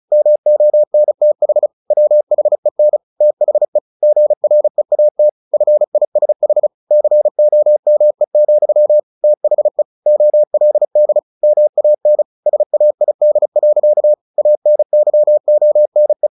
Morse code, heard as a keyed tone, about 34 wpm, 600 Hz. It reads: MONTH WHEN THE GREAT FISH COME, THE OLD MAN SAID. ANYONE
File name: the old man and the sea / the old man and the sea 1 35wpm